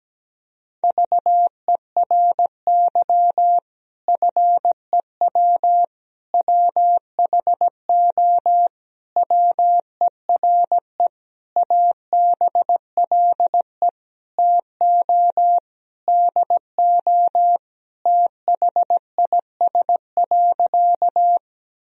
Morse code, heard as a keyed tone, about 17 wpm, 700 Hz. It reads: VERY FEW WHO WERE ABLE TO DO THIS.
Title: Sentences from Top 1000 Words - Review Part 2 - 17wpm